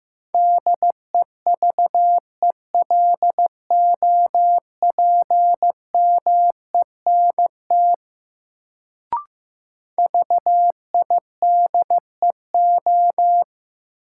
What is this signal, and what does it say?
Morse code, heard as a keyed tone, about 15 wpm, 700 Hz. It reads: DEVELOPMENT VIDEO
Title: Sentences from Top 1000 Words - Review Part 4 - New Word Drill - 15wpm